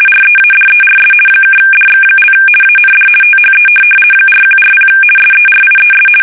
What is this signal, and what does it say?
RTTY, about 50 baud. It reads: READERS (DX) AND THE 4140 SERIES
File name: fsk850-50B / teletype-corp